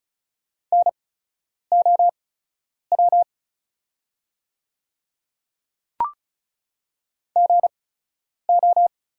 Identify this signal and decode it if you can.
Morse code, heard as a keyed tone, about 36 wpm, spaced as for 10 wpm, 690 Hz. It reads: NOW GO
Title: 8x Character Spacing - Top 100 Words - 35wpm